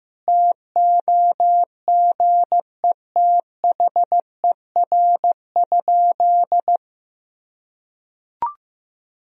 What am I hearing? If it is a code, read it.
Morse code, heard as a keyed tone, about 15 wpm, 700 Hz. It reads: TOGETHER?
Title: Sentences from Top 1000 Words - Review Part 4 - 15wpm